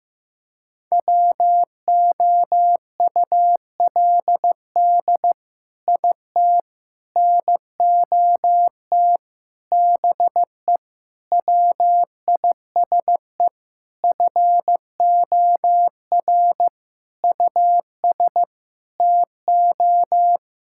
Morse code, heard as a keyed tone, about 16 wpm, 700 Hz. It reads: WOULD IT NOT BE WISE FOR US TO
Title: Sentences from Top 1600 Words 15wpm